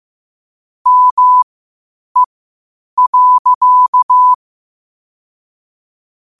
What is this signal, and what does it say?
Morse code, heard as a keyed tone, about 15 wpm, spaced as for 8 wpm, 1000 Hz. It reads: ME.
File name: qotdfiles / 8wpm